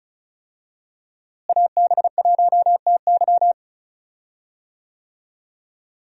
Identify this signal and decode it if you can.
Morse code, heard as a keyed tone, about 36 wpm, 700 Hz. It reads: AB1TY
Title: USA Call Signs - 5 Characters - Repeated 2x - 35wpm